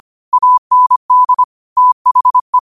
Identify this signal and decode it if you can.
Morse code, heard as a keyed tone, about 25 wpm, 1000 Hz. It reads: AND THE
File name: qotdfiles / 25wpm